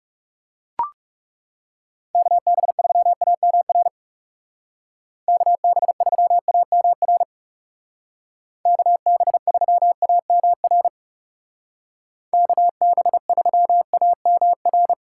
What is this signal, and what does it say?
Morse code, heard as a keyed tone, about 36 wpm, 700 Hz. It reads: KB3AMR KB3AMR KB3AMR KB3AMR
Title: Speed Racing - USA 2x3 Call Signs 30wpm